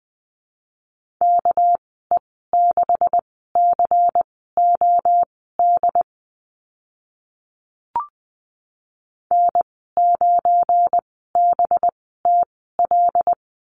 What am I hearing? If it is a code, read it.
Morse code, heard as a keyed tone, about 20 wpm, spaced as for 14 wpm, 700 Hz. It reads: KE6COD N9BTL
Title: 207 Lesson 33g - Letter j - Call Signs 20wpm